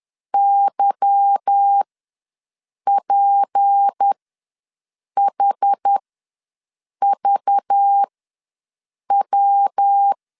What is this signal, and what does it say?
Morse code, heard as a keyed tone, about 10 wpm, spaced as for 6 wpm, 800 Hz. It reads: YPHVW